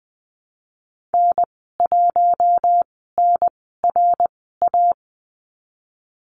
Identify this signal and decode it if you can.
Morse code, heard as a keyed tone, about 20 wpm, spaced as for 15 wpm, 700 Hz. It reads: N1NRA
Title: 055 Lesson 10g - Letter r - Call Signs 20wpm